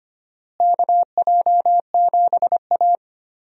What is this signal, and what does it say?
Morse code, heard as a keyed tone, about 25 wpm, 700 Hz. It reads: KJ7A
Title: USA Call Signs - 4 Characters - Repeated 3x - 25wpm